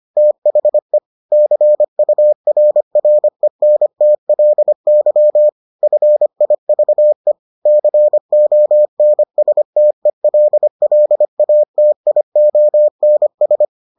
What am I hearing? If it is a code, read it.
Morse code, heard as a keyed tone, about 25 wpm, 600 Hz. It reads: THE CURRENTLY FIVE CONSTELLATIONS